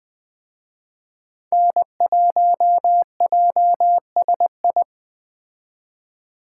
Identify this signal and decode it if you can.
Morse code, heard as a keyed tone, about 20 wpm, 700 Hz. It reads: N1JSI